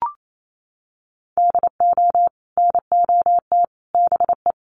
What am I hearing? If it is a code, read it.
Morse code, heard as a keyed tone, about 28 wpm, 700 Hz. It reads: DO NOT BE
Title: Sets of 5 Words Encore2 28wpm